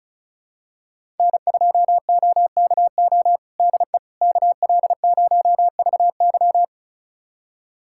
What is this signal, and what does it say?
Morse code, heard as a keyed tone, about 34 wpm, 700 Hz. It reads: N2OKO DE KL0VY
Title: USA Callsign DE Callsign 35wpm